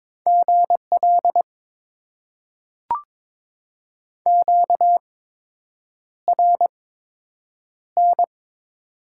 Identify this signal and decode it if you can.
Morse code, heard as a keyed tone, about 22 wpm, spaced as for 8 wpm, 700 Hz. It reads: GL QRN
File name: Top 90 QSO Elements - 8x Character Spacing - 22wpm